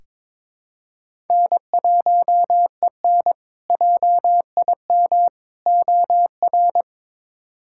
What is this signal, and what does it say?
Morse code, heard as a keyed tone, about 22 wpm, 700 Hz. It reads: N1EN JIM OR
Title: K1USN SST QSO Practice - 22wpm